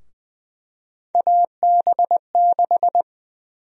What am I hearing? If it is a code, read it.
Morse code, heard as a keyed tone, about 21 wpm, 700 Hz. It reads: AB6
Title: USA Call Sign Prefixes - Repeated 2x - 20wpm